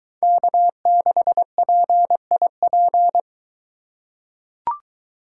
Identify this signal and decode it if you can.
Morse code, heard as a keyed tone, about 22 wpm, 700 Hz. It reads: K6PIP